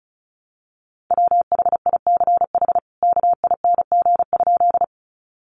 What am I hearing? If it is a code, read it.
Morse code, heard as a keyed tone, about 34 wpm, 700 Hz. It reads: WHICH KING?
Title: Sentences from Top 400 Words 35wpm